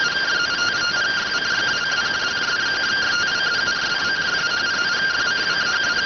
Two-tone teleprinter signal, about 170 Hz shift.